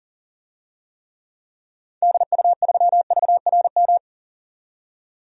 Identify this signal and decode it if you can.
Morse code, heard as a keyed tone, about 40 wpm, 700 Hz. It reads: DU3VRM
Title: DXCC-100 Call Signs - 6 Characters - 40wpm